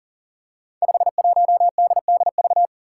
Morse code, heard as a keyed tone, about 40 wpm, 700 Hz. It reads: 51DDV
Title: Five Challenging Letter-Number Combinations 40wpm